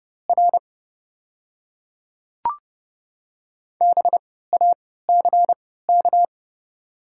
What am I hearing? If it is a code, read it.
Morse code, heard as a keyed tone, about 30 wpm, spaced as for 17 wpm, 700 Hz. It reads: R BACK
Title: 3x Character Spacing - Top 200 Words - 30wpm